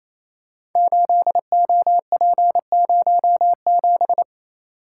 Morse code, heard as a keyed tone, about 28 wpm, 700 Hz. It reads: 8OP07
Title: Code Groups - 5 Letter:Numbers - 28wpm